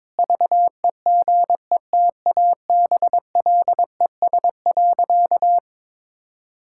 Morse code, heard as a keyed tone, about 22 wpm, 700 Hz. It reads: VEGETABLES.
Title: Pangram Challenge v2 - 22wpm